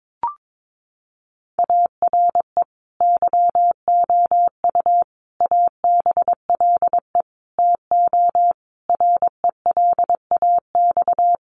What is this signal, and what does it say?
Morse code, heard as a keyed tone, about 22 wpm, 700 Hz. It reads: ARE YOU ABLE TO RELAX
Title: Sentences from Top 1400 Words 22wpm